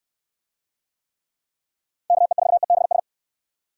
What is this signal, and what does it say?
Morse code, heard as a keyed tone, about 60 wpm, 700 Hz. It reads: KE4EBS